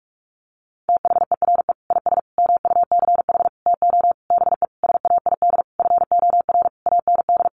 Morse code, heard as a keyed tone, about 44 wpm, 700 Hz. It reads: THERE IS MUCH TO BE SAID FOR AND
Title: Sentences from Top 1300 Words 45wpm